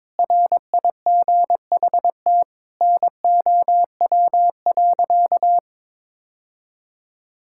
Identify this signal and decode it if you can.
Morse code, heard as a keyed tone, about 22 wpm, 700 Hz. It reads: RIGHT NOW.
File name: practice / 22wpm / Sentences with 2-Word Phrases - 22wpm